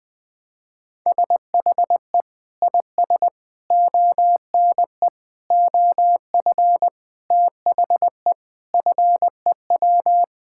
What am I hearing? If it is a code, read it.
Morse code, heard as a keyed tone, about 20 wpm, 700 Hz. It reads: SHE IS ONE OF THE FEW
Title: Sentences with 5-Word Phrases - Spoken Beginning - 20wpm